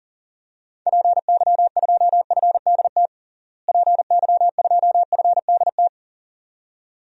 Morse code, heard as a keyed tone, about 40 wpm, 700 Hz. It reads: PY2FDT PY2FDT
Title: DXCC-100 Call Signs - 6 Characters - Repeated 2x - 40wpm